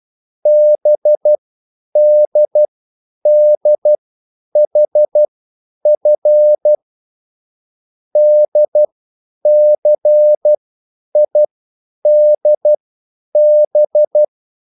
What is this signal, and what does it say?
Morse code, heard as a keyed tone, about 12 wpm, spaced as for 9 wpm, 600 Hz. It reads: BDDHF DCIDB